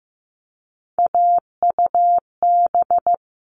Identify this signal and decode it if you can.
Morse code, heard as a keyed tone, about 15 wpm, 700 Hz. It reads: AUB
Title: All Cut Numbers - 3-Digit 15wpm